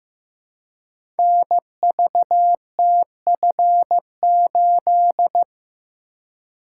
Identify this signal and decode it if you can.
Morse code, heard as a keyed tone, about 15 wpm, 700 Hz. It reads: NVTF8
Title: Code Groups - 5 Letter:Numbers - 15wpm